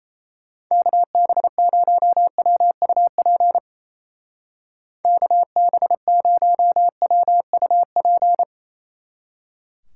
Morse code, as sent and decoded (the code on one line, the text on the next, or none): -.- -... ----- .-- ..- .--. / -.- -... ----- .-- ..- .--.
KB0WUP KB0WUP